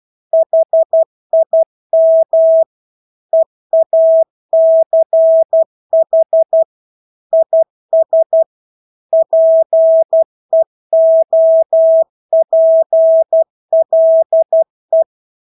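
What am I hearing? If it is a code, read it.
Morse code, heard as a keyed tone, about 12 wpm, 650 Hz. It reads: HIM EACH IS PEOPLE
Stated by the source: Morse practice of English words at 12 wpm.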